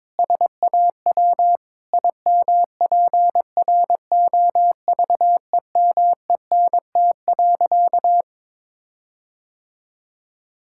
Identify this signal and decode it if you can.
Morse code, heard as a keyed tone, about 22 wpm, 700 Hz. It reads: SAW IMPROVEMENT.
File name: practice / 22wpm / Sentences with 8-Word Phrases - 22wpm